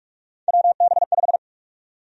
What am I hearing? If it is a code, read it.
Morse code, heard as a keyed tone, about 46 wpm, 700 Hz. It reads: WB5
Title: USA Call Sign Prefixes - 45wpm